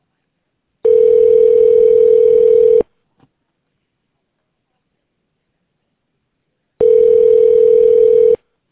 Ringback tone (North America), 440 + 480 Hz.